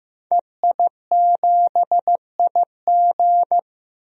Morse code, heard as a keyed tone, about 16 wpm, 700 Hz. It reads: EI7IG
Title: DXCC-100 Call Signs - 5 Characters - Repeated 2x - 15wpm